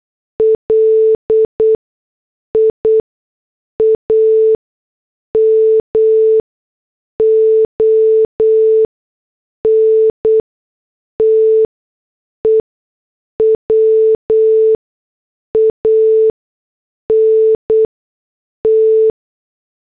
Morse code, heard as a keyed tone, about 8 wpm, spaced as for 6 wpm, 440 Hz. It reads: LIAM ONTE WANT
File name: morse / headlines